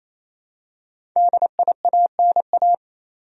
Morse code, heard as a keyed tone, about 28 wpm, 700 Hz. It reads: DIANA
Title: USA Common Names - Max 5 Letters - 28wpm